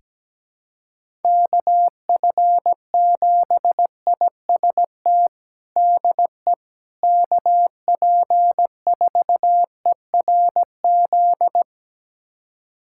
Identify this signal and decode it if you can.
Morse code, heard as a keyed tone, about 17 wpm, 700 Hz. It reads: KF7IST DE KP4ERZ
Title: USA Callsign DE Callsign - 6 Characters - 17wpm